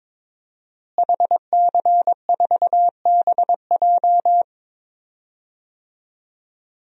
Morse code, heard as a keyed tone, about 22 wpm, 700 Hz. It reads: HC4BJ
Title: DXCC-100 Call Signs - 5 Characters - Repeated 2x - 22wpm